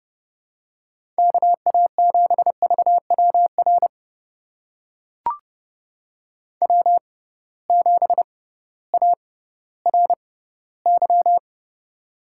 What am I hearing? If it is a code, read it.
Morse code, heard as a keyed tone, about 31 wpm, spaced as for 16 wpm, 700 Hz. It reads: KA7VWR W7ARY
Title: USA Call Signs - 6x Character Spacing - 30wpm